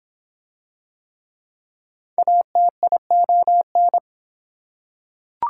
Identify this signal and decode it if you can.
Morse code, heard as a keyed tone, about 23 wpm, 700 Hz. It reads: ATION